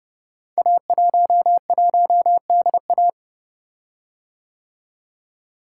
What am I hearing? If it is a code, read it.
Morse code, heard as a keyed tone, about 30 wpm, 700 Hz. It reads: A11DA
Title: Code Groups - 5 Letter:Numbers - 30wpm